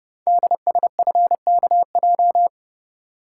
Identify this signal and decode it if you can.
Morse code, heard as a keyed tone, about 29 wpm, 690 Hz. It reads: DSFKJ
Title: Code Groups - 5 Letters - 30wpm